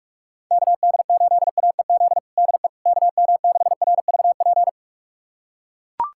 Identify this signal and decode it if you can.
Morse code, heard as a keyed tone, about 41 wpm, 700 Hz. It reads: KD8AEZ DE KM6RVP
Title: USA Callsign DE Callsign - 6 Characters - 45wpm